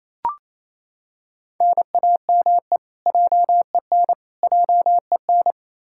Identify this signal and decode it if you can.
Morse code, heard as a keyed tone, about 28 wpm, 700 Hz. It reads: NAME JEN JEN
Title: K1USN SST QSO Practice 28wpm